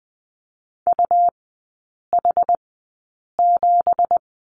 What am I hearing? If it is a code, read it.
Morse code, heard as a keyed tone, about 19 wpm, spaced as for 8 wpm, 700 Hz. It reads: UH7
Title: 166 Lesson 27e - Number 7 - Three Characters 20wpm